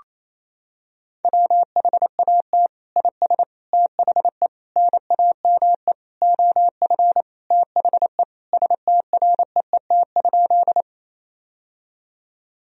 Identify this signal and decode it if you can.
Morse code, heard as a keyed tone, about 28 wpm, 700 Hz. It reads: WHAT IS THE NAME OF THE STREET?
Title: Sentences from Top 1000 Words - Review Part 3 - 28wpm